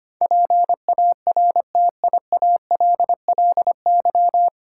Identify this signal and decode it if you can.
Morse code, heard as a keyed tone, about 25 wpm, 700 Hz. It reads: PARTIALLY
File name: Nine Letter Words - 25wpm